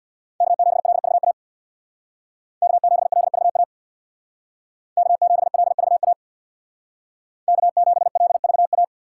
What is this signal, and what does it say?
Morse code, heard as a keyed tone, about 64 wpm, 700 Hz. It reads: K7LVA K7LVA K7LVA K7LVA